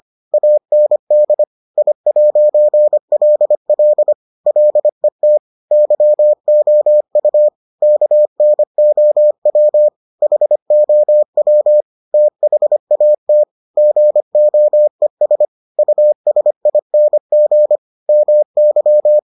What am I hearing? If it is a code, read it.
Morse code, heard as a keyed tone, about 25 wpm, 600 Hz. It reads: AND I'LL LET YOU KNOW HOW THAT GOES USING MY